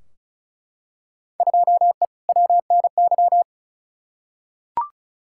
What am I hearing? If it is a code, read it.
Morse code, heard as a keyed tone, about 36 wpm, 700 Hz. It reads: 2E WNY
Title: Field Day Exchanges Encore - 35wpm